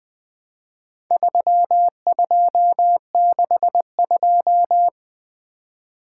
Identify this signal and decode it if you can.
Morse code, heard as a keyed tone, about 23 wpm, 700 Hz. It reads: 3262
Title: Speed-Racing - Four Letter-Number Combinations - 20wpm